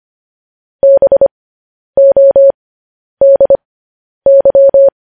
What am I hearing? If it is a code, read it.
Morse code, heard as a keyed tone, about 25 wpm, spaced as for 10 wpm, 550 Hz. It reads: BODY
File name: podcast / News headlines-1x-25x10